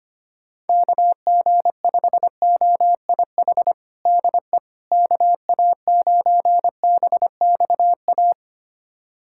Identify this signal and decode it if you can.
Morse code, heard as a keyed tone, about 25 wpm, 700 Hz. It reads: KG5OIH DE KA9BXA
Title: USA Callsign DE Callsign - 6 Characters - 25wpm